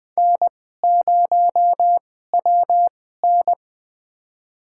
Morse code, heard as a keyed tone, about 20 wpm, spaced as for 14 wpm, 700 Hz. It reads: N0WN
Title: USA Call Signs - 4 Characters - 2x Character Spacing - 20wpm